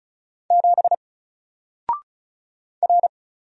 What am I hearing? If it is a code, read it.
Morse code, heard as a keyed tone, about 36 wpm, 700 Hz. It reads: 7 R
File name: Single-Letter-Number-35wpm